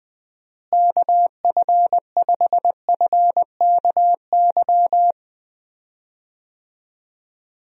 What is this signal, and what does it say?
Morse code, heard as a keyed tone, about 20 wpm, 700 Hz. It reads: KF5FKY
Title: USA Call Signs - 6 Characters - Repeated 2x - 20wpm